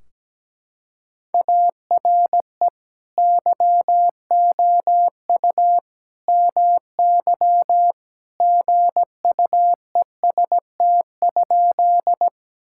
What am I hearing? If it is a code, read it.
Morse code, heard as a keyed tone, about 17 wpm, 700 Hz. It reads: ARE YOU MY GUEST?
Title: Sentences from Top 1600 Words 17wpm